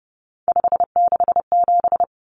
Morse code, heard as a keyed tone, about 30 wpm, 700 Hz. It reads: 567
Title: Numbers - 3-Digit 30wpm